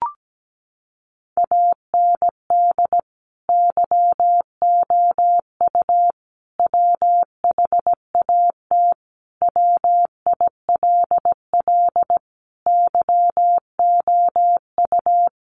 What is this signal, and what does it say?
Morse code, heard as a keyed tone, about 17 wpm, 700 Hz. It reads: AND YOU WHAT WILL YOU
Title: Sentences from Top 600 Words 17wpm